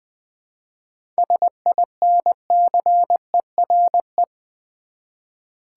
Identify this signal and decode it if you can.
Morse code, heard as a keyed tone, about 20 wpm, 690 Hz. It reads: SINCERE
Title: 7 Letter Words - First 3 Letters Given - 20wpm